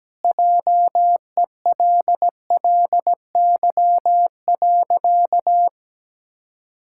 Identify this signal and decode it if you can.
Morse code, heard as a keyed tone, about 17 wpm, 700 Hz. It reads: JELLY.
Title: Pangram Challenge v2 - 17wpm